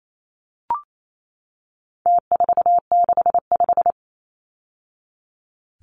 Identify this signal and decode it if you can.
Morse code, heard as a keyed tone, about 28 wpm, 700 Hz. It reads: T465
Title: Numbers - Contest Style - 4-Digit 28wpm